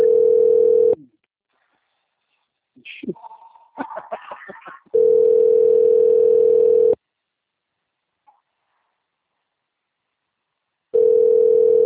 Ringback tone (North America), 440 + 480 Hz.